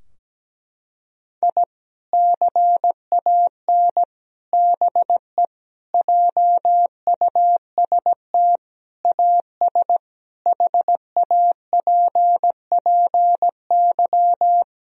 Morse code, heard as a keyed tone, about 17 wpm, 700 Hz. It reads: I CAN BE JUST AS HAPPY
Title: Sentences from Top 1000 Words - Review Part 3 - 17wpm